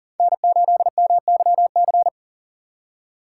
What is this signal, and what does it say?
Morse code, heard as a keyed tone, about 55 wpm, spaced as for 47 wpm, 700 Hz. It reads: N8MYC
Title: Speed Racing - USA Call Signs 40wpm